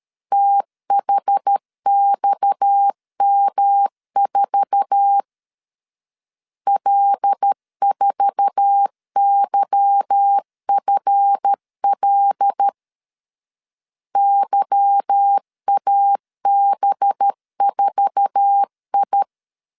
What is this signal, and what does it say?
Morse code, heard as a keyed tone, about 13 wpm, 800 Hz. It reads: THXM4 L4YFL YAB4I
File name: LEZIONI / 8 YJ45 / 8 YJ4 12wpm 2151